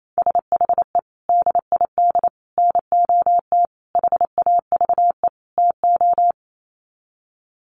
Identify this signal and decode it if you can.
Morse code, heard as a keyed tone, about 28 wpm, 700 Hz. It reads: SHE DID NOT HAVE TO
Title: Sets of 5 Words Encore2 28wpm